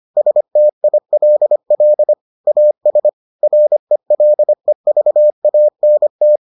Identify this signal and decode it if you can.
Morse code, heard as a keyed tone, about 25 wpm, 600 Hz. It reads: STILL AS RELEVANT